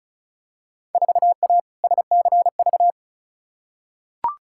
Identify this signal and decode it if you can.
Morse code, heard as a keyed tone, about 36 wpm, 700 Hz. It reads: 4A SCV
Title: Field Day Exchanges - 35wpm